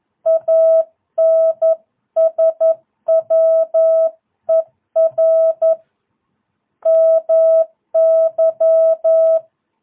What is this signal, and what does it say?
Morse code, heard as a keyed tone, about 11 wpm, 660 Hz. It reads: ANSWER MY
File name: jh8cbh / 2022cw